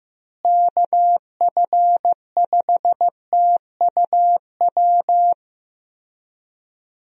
Morse code, heard as a keyed tone, about 16 wpm, 700 Hz. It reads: KF5TUW
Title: USA Call Signs - 6 Characters - Repeated 3x - 15wpm